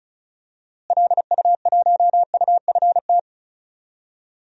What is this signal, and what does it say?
Morse code, heard as a keyed tone, about 36 wpm, 700 Hz. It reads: LU1UFT
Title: DXCC-100 Call Signs - 6 Characters - Repeated 2x - 35wpm